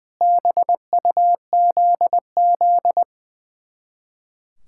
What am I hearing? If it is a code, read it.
Morse code, heard as a keyed tone, about 21 wpm, 700 Hz. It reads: BUZZ
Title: Speed-Racing Common QSO Phrases Encore - 20wpm